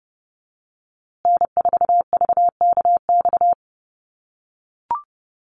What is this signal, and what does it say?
Morse code, heard as a keyed tone, about 30 wpm, 700 Hz. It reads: N4VKX
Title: USA Call Signs Repeated 3x 30wpm